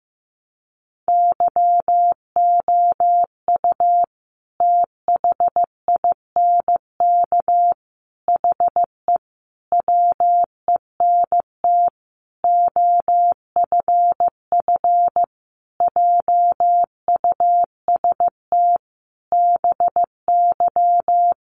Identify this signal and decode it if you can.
Morse code, heard as a keyed tone, about 15 wpm, 700 Hz. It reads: YOU THINK HE WENT OFF JUST BY
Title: Sentences from Top 800 Words 15wpm